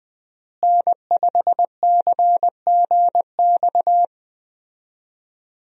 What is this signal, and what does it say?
Morse code, heard as a keyed tone, about 20 wpm, 700 Hz. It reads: N5CGX
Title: USA Call Signs - 5 Characters - Repeated 3x - 20wpm